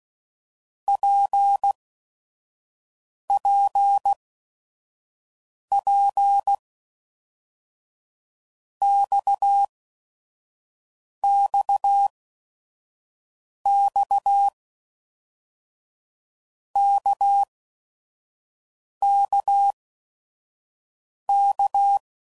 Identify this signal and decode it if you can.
Morse code, heard as a keyed tone, about 16 wpm, spaced as for 5 wpm, 800 Hz. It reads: PPP XXX KKK
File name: Basic Letter Groups By7Groups / RKPX / RKPX@5 f15